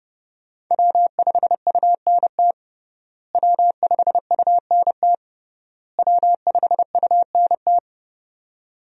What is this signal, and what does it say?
Morse code, heard as a keyed tone, about 32 wpm, 700 Hz. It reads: W5UNT W5UNT W5UNT
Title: USA Call Signs - 5 Characters - Repeated 3x - 30wpm